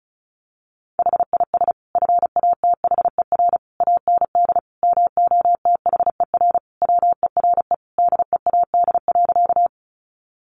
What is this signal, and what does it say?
Morse code, heard as a keyed tone, about 34 wpm, 700 Hz. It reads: HIS FATHER AND MOTHER WERE DEAD.
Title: Sentences from Top 1500 Words 35wpm